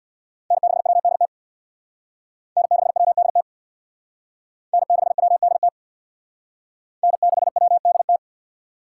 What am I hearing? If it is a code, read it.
Morse code, heard as a keyed tone, about 62 wpm, 700 Hz. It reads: N6WDT N6WDT N6WDT N6WDT
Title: Speed Racing - USA 1x3 Call Signs 50wpm